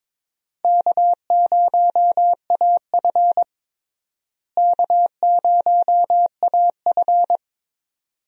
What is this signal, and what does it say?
Morse code, heard as a keyed tone, about 21 wpm, 700 Hz. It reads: K0AF K0AF
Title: USA Call Signs - 4 Characters - Repeated 2x - 22wpm